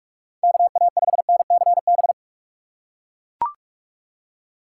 Morse code, heard as a keyed tone, about 43 wpm, 700 Hz. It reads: KA5NCB
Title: USA Call Signs - 6 Characters - 4x Character Spacing - 45wpm